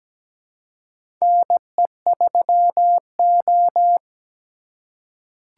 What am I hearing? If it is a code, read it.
Morse code, heard as a keyed tone, about 17 wpm, 700 Hz. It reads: NE3O
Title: USA Call Signs - 4x Character Spacing - 17wpm